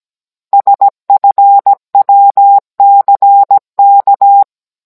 Morse code, heard as a keyed tone, about 17 wpm, 800 Hz. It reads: SFWCK
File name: Cours-NR18-16mm